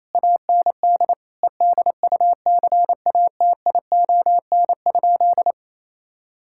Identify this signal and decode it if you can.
Morse code, heard as a keyed tone, about 28 wpm, 700 Hz. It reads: AND EDUCATION?
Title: Sentences from Top 1000 Words - Review Part 6 - 28wpm